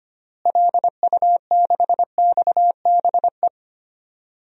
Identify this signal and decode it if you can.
Morse code, heard as a keyed tone, about 25 wpm, 700 Hz. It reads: LU6XBE
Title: DXCC-100 Call Signs - 6 Characters - Repeated 2x - 25wpm